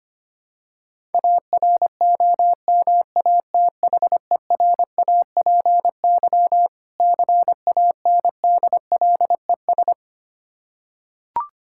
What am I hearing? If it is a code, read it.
Morse code, heard as a keyed tone, about 25 wpm, 700 Hz. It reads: AROMATHERAPY CANDLES
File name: practice / 25wpm / Difficult Words with Hints - 25wpm